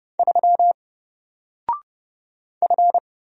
Morse code, heard as a keyed tone, about 31 wpm, 700 Hz. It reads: 3 F
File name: Single-Letter-Number-30wpm